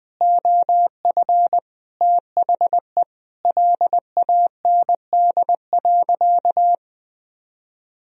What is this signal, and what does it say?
Morse code, heard as a keyed tone, about 20 wpm, 700 Hz. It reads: OF THE LAND.
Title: Sentences from Top 1000 Words - Review Part 3 - 20wpm